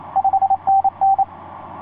Morse code, heard as a keyed tone, about 28 wpm, 780 Hz. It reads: 5NN
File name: 6w-gm4fdm f6bee etc 15m cw 30jan2010 1502